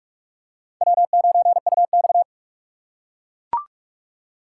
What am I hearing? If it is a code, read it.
Morse code, heard as a keyed tone, about 42 wpm, 700 Hz. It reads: W9UX